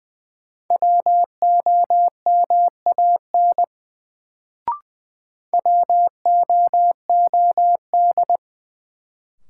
Morse code, heard as a keyed tone, about 21 wpm, 700 Hz. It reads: WOMAN WOOD
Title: Sentences from Top 1000 Words - Review Part 2 - New Word Drill - 20wpm